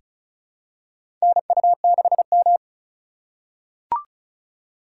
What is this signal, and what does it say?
Morse code, heard as a keyed tone, about 34 wpm, 700 Hz. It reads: NU6M
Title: USA Call Signs - 2x Character Spacing - 35wpm